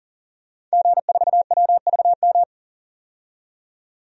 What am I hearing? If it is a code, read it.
Morse code, heard as a keyed tone, about 37 wpm, 700 Hz. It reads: G4WVM
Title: DXCC-100 Call Signs - 5 Characters - 40wpm